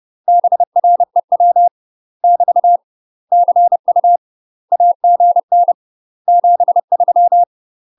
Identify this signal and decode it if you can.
Morse code, heard as a keyed tone, about 30 wpm, 700 Hz. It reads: BREW <BT> CU AGN 73
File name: UK MASTER QSOs TRAINING / 30WPM / UK MASTER QSOs 0016